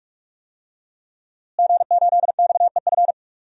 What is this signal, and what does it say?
Morse code, heard as a keyed tone, about 46 wpm, 700 Hz. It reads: G8XEF
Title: DXCC-100 Call Signs - 5 Characters - Repeated 2x - 45wpm